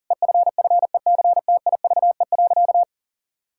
Morse code, heard as a keyed tone, about 40 wpm, 700 Hz. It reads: EFFECTIVE.
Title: Sentences from Top 1000 Words - Review Part 6 - 40wpm